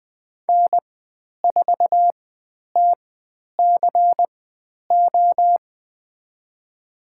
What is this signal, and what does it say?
Morse code, heard as a keyed tone, about 20 wpm, spaced as for 10 wpm, 700 Hz. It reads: N4TCO
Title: 095 Lesson 16g - Letter c - Call Signs 20wpm